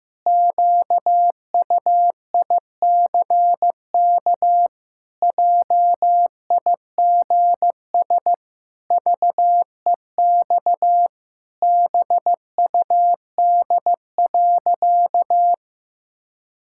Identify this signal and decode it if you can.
Morse code, heard as a keyed tone, about 15 wpm, 700 Hz. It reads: QUICK JIGS VEX BUD.
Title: Pangram Challenge v2 - 15wpm